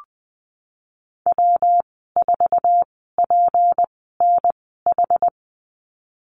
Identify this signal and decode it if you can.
Morse code, heard as a keyed tone, about 20 wpm, spaced as for 14 wpm, 700 Hz. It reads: W4PNH
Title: 167 Lesson 27g - Number 7 - Call Signs 20wpm